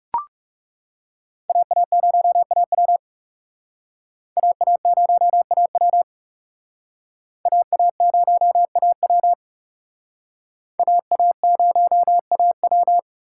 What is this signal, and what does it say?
Morse code, heard as a keyed tone, about 37 wpm, 700 Hz. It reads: AA0AW AA0AW AA0AW AA0AW
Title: Speed Racing - USA 2x2 Call Signs 30wpm